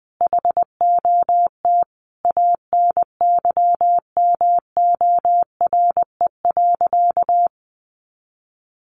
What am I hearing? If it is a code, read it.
Morse code, heard as a keyed tone, about 20 wpm, 700 Hz. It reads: HOT ANYMORE.
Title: Additional Easy Phrases 20wpm